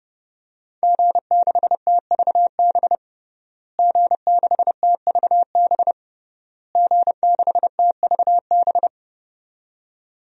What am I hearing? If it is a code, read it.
Morse code, heard as a keyed tone, about 32 wpm, 700 Hz. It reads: G6TVB G6TVB G6TVB
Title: DXCC-100 Call Signs - 5 Characters - Repeated 3x - 30wpm